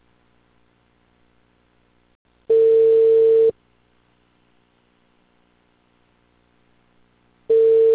Ringback tone (China, 450 Hz).